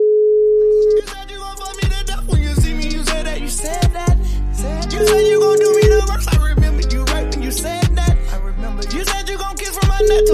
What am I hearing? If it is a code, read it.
Ringback tone (Europe, 425 Hz).